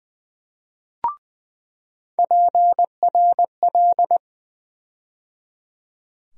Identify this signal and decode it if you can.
Morse code, heard as a keyed tone, about 20 wpm, 700 Hz. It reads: PRL
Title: USA Call Sign Postfixes - 3 Letters 20wpm